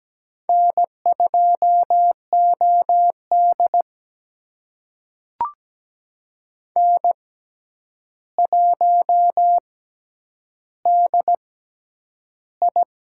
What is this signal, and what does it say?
Morse code, heard as a keyed tone, about 17 wpm, spaced as for 9 wpm, 700 Hz. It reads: N2OD N1DI
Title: USA Call Signs - 4 Characters - 6x Character Spacing - 17wpm